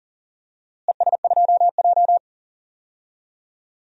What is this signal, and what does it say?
Morse code, heard as a keyed tone, about 40 wpm, 700 Hz. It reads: ES2J
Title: DXCC-100 Call Signs - 4 Characters - Repeated 2x - 40wpm